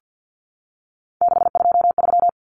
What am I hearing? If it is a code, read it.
Morse code, heard as a keyed tone, about 48 wpm, 700 Hz. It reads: 623
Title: Common Cut Numbers - 3-Digit 50wpm